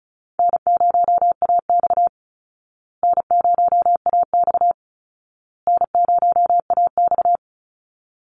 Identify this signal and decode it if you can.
Morse code, heard as a keyed tone, about 34 wpm, 700 Hz. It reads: N0AX N0AX N0AX
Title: Easy Call Signs Repeated 3x - 35wpm